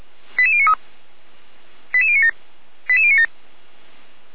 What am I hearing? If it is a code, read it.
Five-tone selective call (ZVEI), heard as quick runs of five tones, tones of about 70 ms. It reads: 88093 88097 88097
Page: /455/SDR/ZVEI/88057/